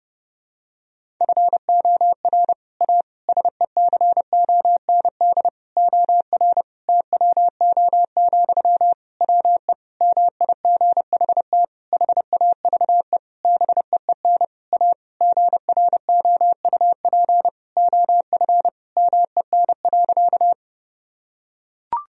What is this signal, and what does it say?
Morse code, heard as a keyed tone, about 32 wpm, 700 Hz. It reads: FOR A SECOND OR TWO, WE MIGHT HAVE BEEN A GROUP OF MEN.
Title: Sentences from Top 1000 Words - Review Part 1 - 30wpm